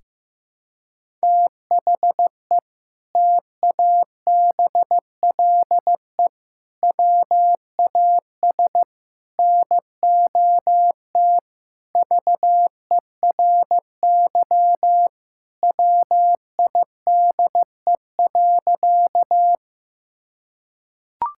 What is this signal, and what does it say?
Morse code, heard as a keyed tone, about 16 wpm, 700 Hz. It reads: THE TABLE WAS NOT VERY WIDE.
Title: Sentences from Top 1000 Words - Review Part 6 - 15wpm